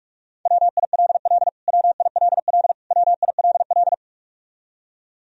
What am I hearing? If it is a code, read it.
Morse code, heard as a keyed tone, about 46 wpm, 700 Hz. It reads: WILL WILL WILL
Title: Top 100 Words Repeated 45wpm